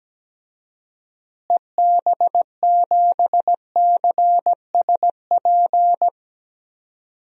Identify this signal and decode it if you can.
Morse code, heard as a keyed tone, about 17 wpm, 700 Hz. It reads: EB7CSP
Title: DXCC-100 Call Signs - 6 Characters - Repeated 3x - 17wpm